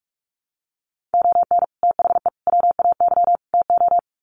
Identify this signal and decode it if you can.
Morse code, heard as a keyed tone, about 44 wpm, 700 Hz. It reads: ON THE WAY TO
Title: Sets of 4 Words 45wpm